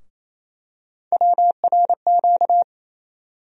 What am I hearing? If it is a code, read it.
Morse code, heard as a keyed tone, about 28 wpm, 700 Hz. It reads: WRQ
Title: USA Call Sign Postfixes - 3 Letters 28wpm